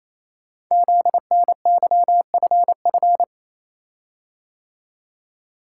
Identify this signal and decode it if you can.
Morse code, heard as a keyed tone, about 29 wpm, 700 Hz. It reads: ZNYFF
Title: Code Groups - 5 Letters - 28wpm